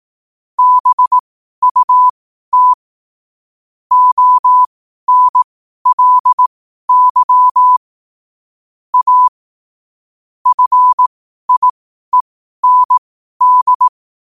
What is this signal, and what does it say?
Morse code, heard as a keyed tone, about 18 wpm, spaced as for 12 wpm, 1000 Hz. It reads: BUT ONLY A FIEND
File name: qotdfiles / 12wpm